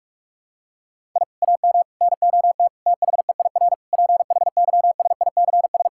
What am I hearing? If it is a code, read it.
Morse code, heard as a keyed tone, about 42 wpm, 700 Hz. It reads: I AM NOT THEIR PHYSICS
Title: Sentences from Top 1000 Words - Review Part 6 - 45wpm